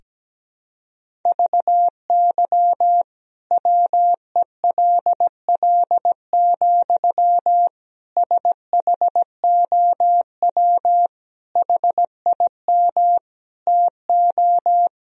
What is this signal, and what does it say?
Morse code, heard as a keyed tone, about 17 wpm, 700 Hz. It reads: VY WELL, SHOW HIM TO
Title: Sentences from Top 200 Words and Abbreviations 17wpm